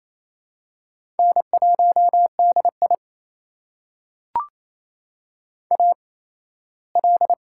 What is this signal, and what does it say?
Morse code, heard as a keyed tone, about 28 wpm, spaced as for 17 wpm, 700 Hz. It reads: N1DI AL
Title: USA Call Signs - 4 Characters - 8x Character Spacing - 28wpm